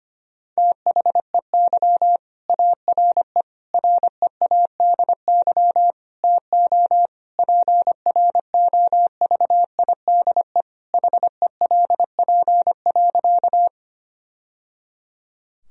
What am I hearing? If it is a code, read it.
Morse code, heard as a keyed tone, about 25 wpm, 700 Hz. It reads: THEY ARE READY TO PROVIDE HELP.
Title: Sentences from Top 1000 Words - Review Part 3 - 25wpm